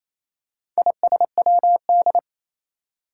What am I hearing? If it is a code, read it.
Morse code, heard as a keyed tone, about 28 wpm, 700 Hz. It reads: ISWD
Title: Four Challenging Letter-Number Combinations 28wpm